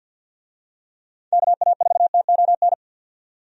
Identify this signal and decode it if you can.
Morse code, heard as a keyed tone, about 53 wpm, 700 Hz. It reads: KA4TON
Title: USA Call Signs - 6 Characters - 50wpm